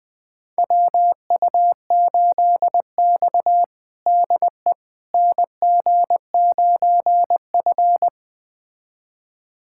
Morse code, heard as a keyed tone, about 20 wpm, 700 Hz. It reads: WU8X DE NG9F
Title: USA Callsign DE Callsign - 4 Characters - 20wpm